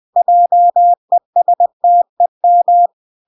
Morse code, heard as a keyed tone, about 20 wpm, 700 Hz. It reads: JESTEM